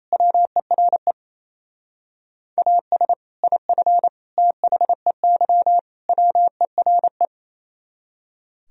Morse code, heard as a keyed tone, about 34 wpm, 700 Hz. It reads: WERE AS IF THEY WERE
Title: Speed-Racing - Sets of 4 Words 28wpm